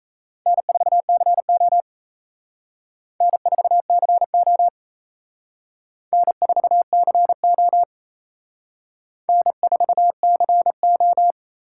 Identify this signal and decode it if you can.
Morse code, heard as a keyed tone, about 35 wpm, 700 Hz. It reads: N4CO N4CO N4CO N4CO